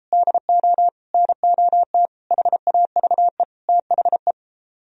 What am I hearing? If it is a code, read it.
Morse code, heard as a keyed tone, about 32 wpm, 700 Hz. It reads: DO NOT HAVE THE